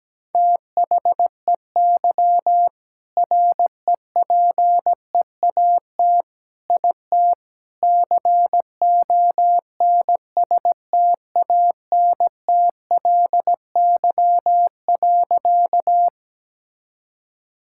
Morse code, heard as a keyed tone, about 17 wpm, 700 Hz. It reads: THEY REPEAT IT CONSTANTLY.
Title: Sentences from Top 1600 Words 17wpm